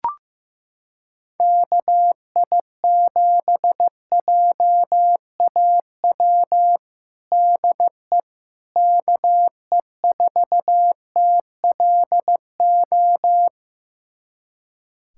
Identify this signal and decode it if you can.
Morse code, heard as a keyed tone, about 15 wpm, 700 Hz. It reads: KI7JAW DE KE4TLO
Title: USA Callsign DE Callsign 15wpm